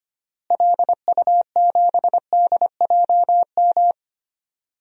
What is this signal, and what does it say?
Morse code, heard as a keyed tone, about 25 wpm, 700 Hz. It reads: LU7DJM
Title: DXCC-100 Call Signs - 6 Characters - Repeated 2x - 25wpm